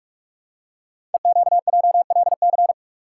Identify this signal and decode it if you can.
Morse code, heard as a keyed tone, about 44 wpm, 700 Hz. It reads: EQJLC
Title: Code Groups - 5 Letter:Numbers - 45wpm